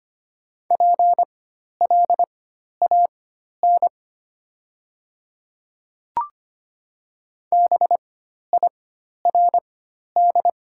Morse code, heard as a keyed tone, about 25 wpm, spaced as for 12 wpm, 690 Hz. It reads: PLAN BIRD
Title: 4x Character Spacing - Top 500 Words - 25wpm